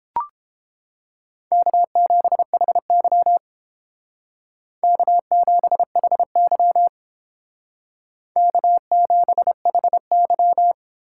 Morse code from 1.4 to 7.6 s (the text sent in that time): K7HY K7HY